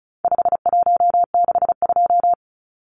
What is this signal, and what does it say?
Morse code, heard as a keyed tone, about 34 wpm, 700 Hz. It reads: 5162
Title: Numbers - 4-Digit 35wpm